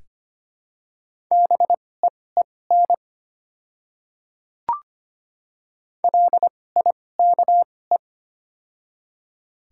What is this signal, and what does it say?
Morse code, heard as a keyed tone, about 25 wpm, spaced as for 18 wpm, 700 Hz. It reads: BEEN LIKE
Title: 2x Character Spacing - Top 100 Words - 25wpm